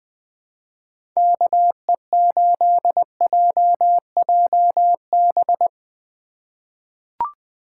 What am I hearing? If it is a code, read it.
Morse code, heard as a keyed tone, about 20 wpm, 700 Hz. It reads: KE8JJB
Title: USA Call Signs - 6 Characters - 4x Character Spacing - 20wpm